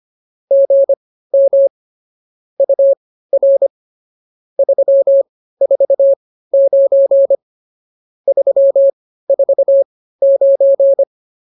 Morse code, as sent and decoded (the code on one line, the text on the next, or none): --. -- / ..- .-. / ...-- ....- ----. / ...-- ....- ----.
GM UR 349 349